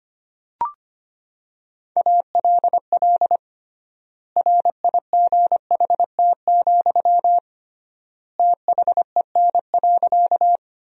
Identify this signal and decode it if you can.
Morse code, heard as a keyed tone, about 25 wpm, 700 Hz. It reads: ALL RIGHT, THEN.
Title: 3x Word-Spacing - Sentences from Top 200 Words - 25wpm